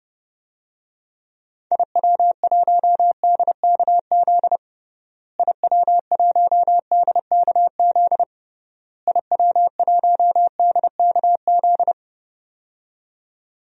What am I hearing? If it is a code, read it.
Morse code, heard as a keyed tone, about 30 wpm, 700 Hz. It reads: IW1DKZ IW1DKZ IW1DKZ
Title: DXCC-100 Call Signs - 6 Characters - Repeated 3x - 30wpm